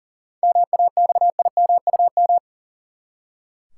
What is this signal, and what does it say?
Morse code, heard as a keyed tone, about 40 wpm, 700 Hz. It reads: MAXIMUM
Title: Seven Letter Words - 40wpm